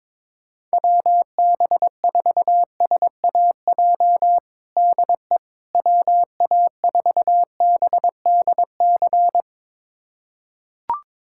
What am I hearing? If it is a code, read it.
Morse code, heard as a keyed tone, about 22 wpm, 700 Hz. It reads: WB4SAJ DE WA4BDC
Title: USA Callsign DE Callsign - 6 Characters - 22wpm